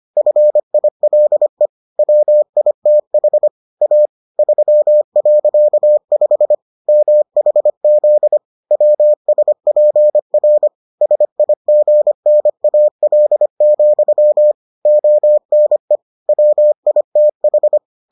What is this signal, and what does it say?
Morse code, heard as a keyed tone, about 25 wpm, 600 Hz. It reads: FILE WITH A 3.5 MHZ WSPR SIGNAL, ONE WITH